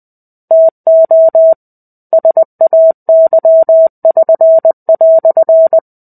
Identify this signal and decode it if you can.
Morse code, heard as a keyed tone, about 20 wpm, 650 Hz. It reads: TO SAY<SN>"
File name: 05-Chapter-05 20 WPM